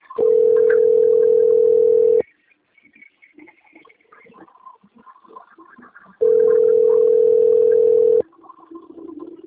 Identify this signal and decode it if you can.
Ringback tone (North America), 440 + 480 Hz.